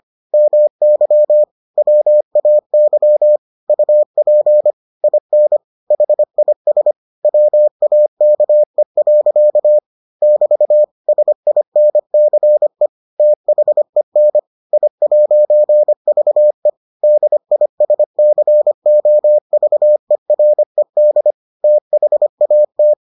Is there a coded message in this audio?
Yes: Morse code from 0.3 to 22.9 s, heard as a keyed tone, about 25 wpm, 600 Hz. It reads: MY WAY UP IN HIS WAKE. <BT> SINCE THEN I'VE DISCOVERED THAT